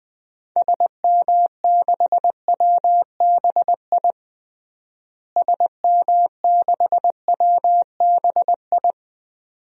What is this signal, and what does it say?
Morse code, heard as a keyed tone, about 20 wpm, 700 Hz. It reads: SM6WBI SM6WBI
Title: DXCC-100 Call Signs - 6 Characters - Repeated 2x - 20wpm